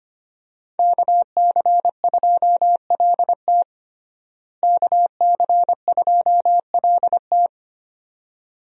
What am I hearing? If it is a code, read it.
Morse code, heard as a keyed tone, about 25 wpm, 700 Hz. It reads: KC2LT KC2LT
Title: USA Call Signs - 5 Characters - Repeated 2x - 25wpm